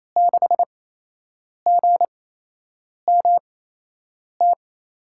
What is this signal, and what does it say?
Morse code, heard as a keyed tone, about 27 wpm, spaced as for 8 wpm, 700 Hz. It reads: 6GMT
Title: USA Call Signs - 5 Characters - 8x Character Spacing - 28wpm